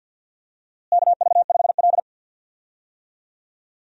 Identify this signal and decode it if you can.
Morse code, heard as a keyed tone, about 51 wpm, 700 Hz. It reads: KV5L
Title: USA Call Signs - 4 Characters - Repeated 3x - 50wpm